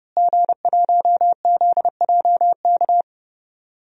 Morse code, heard as a keyed tone, about 31 wpm, 700 Hz. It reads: G1ZJK
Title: DXCC-100 Call Signs - 5 Characters - Repeated 3x - 30wpm